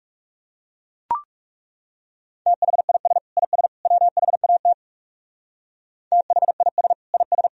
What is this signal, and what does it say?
Morse code, heard as a keyed tone, about 39 wpm, 700 Hz. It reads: THIS IS WHAT THIS IS